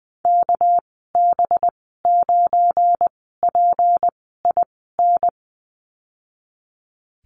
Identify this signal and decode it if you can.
Morse code, heard as a keyed tone, about 20 wpm, spaced as for 14 wpm, 700 Hz. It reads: KB9PIN
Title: 200 Lesson 32g - Letter k - Call Signs 20wpm